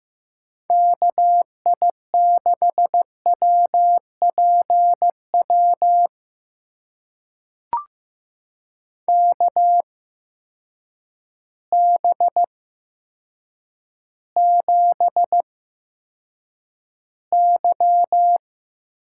Morse code, heard as a keyed tone, about 15 wpm, spaced as for 8 wpm, 700 Hz. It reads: KI6WPW KB7Y
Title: USA Call Signs - 6 Characters - 8x Character Spacing - 15wpm